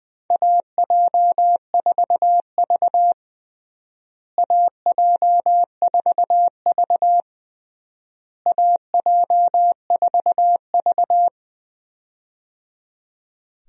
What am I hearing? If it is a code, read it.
Morse code, heard as a keyed tone, about 20 wpm, 700 Hz. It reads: AJ4V AJ4V AJ4V
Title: USA Call Signs - 4 Characters - Repeated 3x - 20wpm